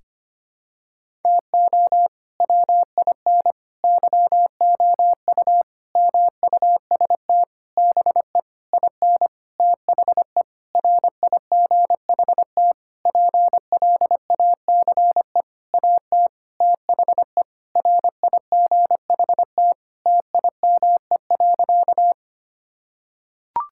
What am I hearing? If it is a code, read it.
Morse code, heard as a keyed tone, about 25 wpm, 700 Hz. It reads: TO WIN YOU MUST BE IN THE RIGHT PLACE AT THE RIGHT TIME.
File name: practice / 25wpm / Sentences with 9-Word Phrases - 25wpm